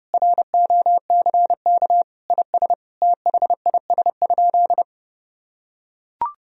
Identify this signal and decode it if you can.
Morse code, heard as a keyed tone, about 32 wpm, 700 Hz. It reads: ROCK IS THIS?
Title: Sentences from Top 1000 Words - Review Part 3 - 30wpm